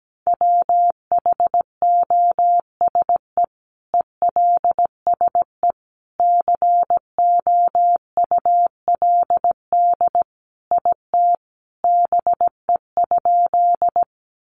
Morse code, heard as a keyed tone, about 17 wpm, 700 Hz. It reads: WHOSE ELSE COULD IT BE?
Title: Sentences from Top 1200 Words 17wpm